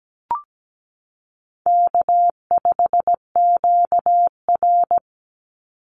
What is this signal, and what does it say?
Morse code, heard as a keyed tone, about 17 wpm, 700 Hz. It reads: K5QR
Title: NA SOTA Chaser Callsigns 17wpm